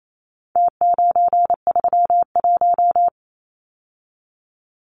Morse code, heard as a keyed tone, about 28 wpm, 700 Hz. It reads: T931
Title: Numbers - Contest Style - 4-Digit 28wpm